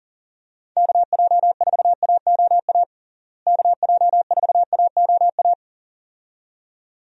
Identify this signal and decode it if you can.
Morse code, heard as a keyed tone, about 37 wpm, 700 Hz. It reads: KJ4AOA KJ4AOA
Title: USA Call Signs - 6 Characters - Repeated 2x - 40wpm